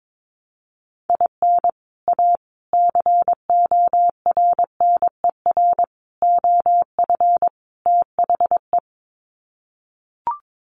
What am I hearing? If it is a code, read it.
Morse code, heard as a keyed tone, about 22 wpm, 700 Hz. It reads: IN A CORNER OF THE
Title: Sets of 5 Words Encore2 22wpm